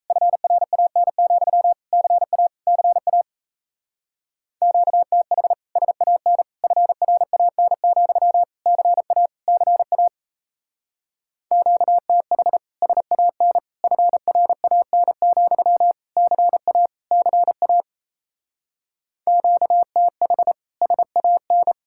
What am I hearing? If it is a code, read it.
Morse code, heard as a keyed tone, about 32 wpm, 700 Hz. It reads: FRAN, CA CA QTH SAN FRAN, CA CA QTH SAN FRAN, CA CA QTH SAN
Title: Speed-Racing Common QSO Phrases - 28wpm